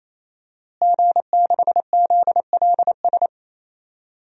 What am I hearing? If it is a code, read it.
Morse code, heard as a keyed tone, about 28 wpm, 700 Hz. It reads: G6ZLS
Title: DXCC-100 Call Signs - 5 Characters - Repeated 3x - 28wpm